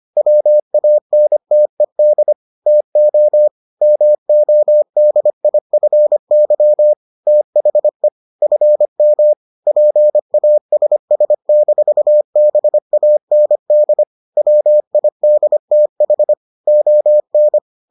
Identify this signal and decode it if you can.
Morse code, heard as a keyed tone, about 25 wpm, 600 Hz. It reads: WANTED TO MODIFY THE FM PASS-BAND WIDTH ON